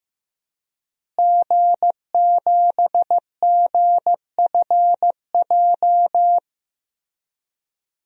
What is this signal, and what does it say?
Morse code, heard as a keyed tone, about 15 wpm, 700 Hz. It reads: G7GFJ
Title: DXCC-100 Call Signs - 5 Characters - Repeated 3x - 15wpm